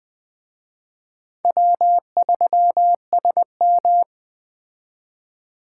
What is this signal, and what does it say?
Morse code, heard as a keyed tone, about 20 wpm, 700 Hz. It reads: W3SM
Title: Speed Racing - USA Call Signs 17wpm